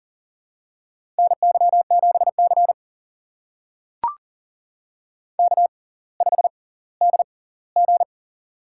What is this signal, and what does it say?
Morse code, heard as a keyed tone, about 38 wpm, spaced as for 20 wpm, 700 Hz. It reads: NY7C K5DG
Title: USA Call Signs - 4 Characters - 6x Character Spacing - 40wpm